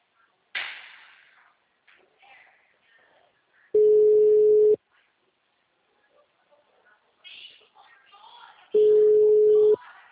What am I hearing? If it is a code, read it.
Ringback tone (Europe, 425 Hz), with speech.